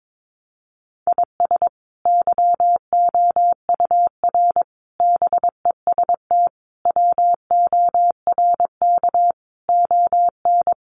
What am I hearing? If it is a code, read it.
Morse code, heard as a keyed tone, about 22 wpm, 700 Hz. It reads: IS YOUR BEST WORK ON
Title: Sentences from Top 1200 Words 22wpm